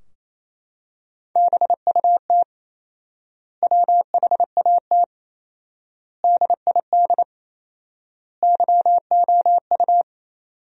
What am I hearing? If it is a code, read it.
Morse code, heard as a keyed tone, about 28 wpm, 700 Hz. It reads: BUT WHAT DID YOU
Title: 4x Word-Spacing - Sentences from Top 300 Words - 28wpm